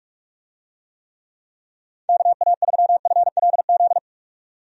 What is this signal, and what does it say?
Morse code, heard as a keyed tone, about 43 wpm, 700 Hz. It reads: KA3FLZ
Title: USA Call Signs - 6 Characters - 45wpm